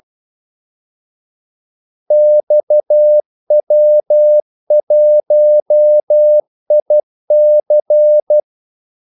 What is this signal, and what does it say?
Morse code, heard as a keyed tone, about 12 wpm, 600 Hz. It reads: XW1IC
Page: 2 ekstra ordmellemrum mellem kaldesignalerne